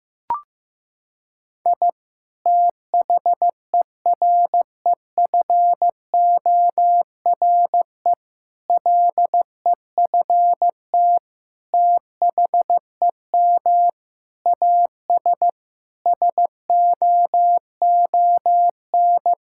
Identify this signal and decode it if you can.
Morse code, heard as a keyed tone, about 16 wpm, 700 Hz. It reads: I THEREFORE LEFT THEM AS SOON
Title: Sentences from Top 1000 Words - Review Part 5 - 15wpm